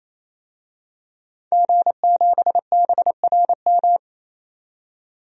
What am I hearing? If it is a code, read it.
Morse code, heard as a keyed tone, about 27 wpm, 700 Hz. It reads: G7BRM
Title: DXCC-100 Call Signs - 5 Characters - 28wpm